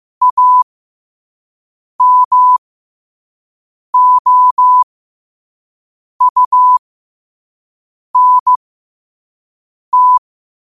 Morse code, heard as a keyed tone, about 15 wpm, spaced as for 5 wpm, 1000 Hz. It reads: AMOUNT
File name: 5wpm